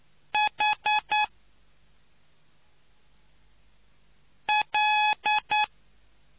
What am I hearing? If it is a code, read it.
Morse code, heard as a keyed tone, about 9 wpm, spaced as for 2 wpm, 850 Hz. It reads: HL